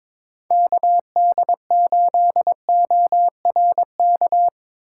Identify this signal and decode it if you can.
Morse code, heard as a keyed tone, about 25 wpm, 700 Hz. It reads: KD8ORK
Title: Speed Racing - USA Call Signs 22wpm